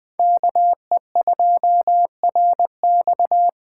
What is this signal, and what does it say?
Morse code, heard as a keyed tone, about 20 wpm, 700 Hz. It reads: KE2RX
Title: Speed Racing - USA 2x2 Call Signs 17wpm